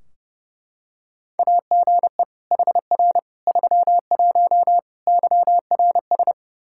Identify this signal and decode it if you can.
Morse code, heard as a keyed tone, about 31 wpm, 700 Hz. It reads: AGE HR 31 YRS
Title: Practice QSO - Exchange 3 - More Info - 30wpm